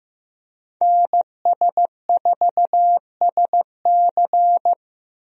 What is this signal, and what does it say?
Morse code, heard as a keyed tone, about 16 wpm, 700 Hz. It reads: NS4SC
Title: USA Call Signs - 5 Characters - Repeated 2x - 15wpm